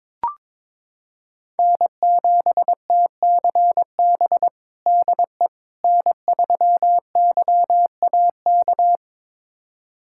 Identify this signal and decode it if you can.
Morse code, heard as a keyed tone, about 21 wpm, 700 Hz. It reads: N7TCB DE N3YAK
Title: USA Callsign DE Callsign - 5 Characters - 22wpm